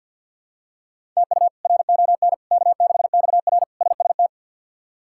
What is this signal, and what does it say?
Morse code, heard as a keyed tone, about 50 wpm, 700 Hz. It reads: TU RON K6XR SST
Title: K1USN SST QSO Practice - 50wpm